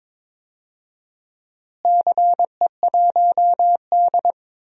Morse code, heard as a keyed tone, about 22 wpm, 700 Hz. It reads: CE1D
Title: DXCC-100 Call Signs - 4 Characters - Repeated 3x - 22wpm